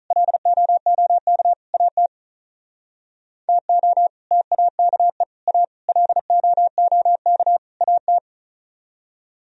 Morse code, heard as a keyed tone, about 44 wpm, 700 Hz. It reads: LOOK AT TO TAKE A LOOK AT